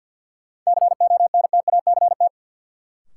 Morse code, heard as a keyed tone, about 49 wpm, 700 Hz. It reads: CONTACT
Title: Sentences from Top 1000 Words - Review Part 7 - New Word Drill - 50wpm